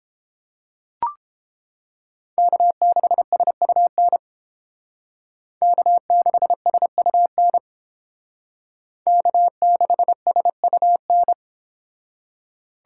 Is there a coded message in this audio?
Yes: Morse code from 2.4 to 11.3 s, heard as a keyed tone, about 29 wpm, 700 Hz. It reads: K6SUN K6SUN K6SUN